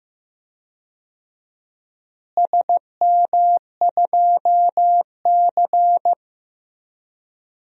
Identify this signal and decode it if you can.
Morse code, heard as a keyed tone, about 15 wpm, 700 Hz. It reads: SM2C
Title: DXCC-100 Call Signs - 4 Characters - Repeated 3x - 15wpm